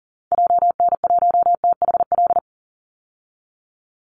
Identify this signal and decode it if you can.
Morse code, heard as a keyed tone, about 40 wpm, 700 Hz. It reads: JN1THL
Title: Call Signs - 40wpm